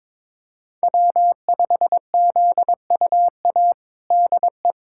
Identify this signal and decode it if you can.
Morse code, heard as a keyed tone, about 22 wpm, 700 Hz. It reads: W5ZUA DE
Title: USA Callsign DE Callsign 22wpm